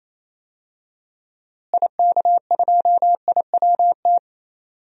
Morse code, heard as a keyed tone, about 27 wpm, 700 Hz. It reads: IK2IWT
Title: DXCC-100 Call Signs - 6 Characters - Repeated 3x - 28wpm